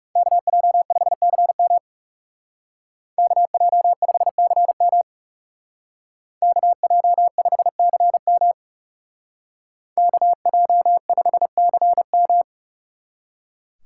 Morse code, heard as a keyed tone, about 38 wpm, 700 Hz. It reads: KJ5CM KJ5CM KJ5CM KJ5CM